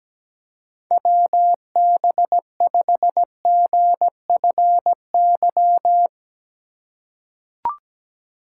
Morse code, heard as a keyed tone, about 17 wpm, 700 Hz. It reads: WB5GFY
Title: USA Call Signs - 6 Characters - 2x Character Spacing - 17wpm